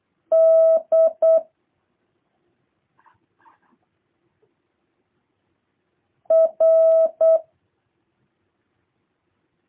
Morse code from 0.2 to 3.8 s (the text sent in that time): D